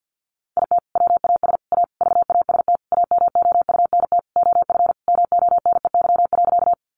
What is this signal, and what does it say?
Morse code, heard as a keyed tone, about 50 wpm, 700 Hz. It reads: IT WAS A VAST AMOUNT OF MONEY.
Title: Sentences from Top 1300 Words 50wpm